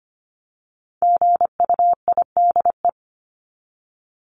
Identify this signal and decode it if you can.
Morse code, heard as a keyed tone, about 25 wpm, 700 Hz. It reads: GUIDE
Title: Top 501-750 Words 25wpm